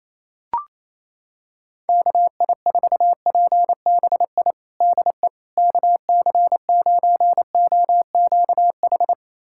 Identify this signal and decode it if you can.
Morse code, heard as a keyed tone, about 28 wpm, 700 Hz. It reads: KI4PBI DE KC9OQH
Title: USA Callsign DE Callsign 28wpm